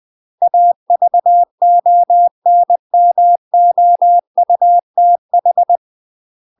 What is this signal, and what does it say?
Morse code, heard as a keyed tone, about 20 wpm, 700 Hz. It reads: AVONMOUTH
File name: MASTER QSOs TRAINING/20WPM